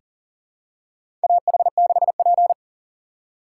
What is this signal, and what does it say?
Morse code, heard as a keyed tone, about 40 wpm, 700 Hz. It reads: AH6P
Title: USA Call Signs - 4 Characters - 3x Character Spacing - 40wpm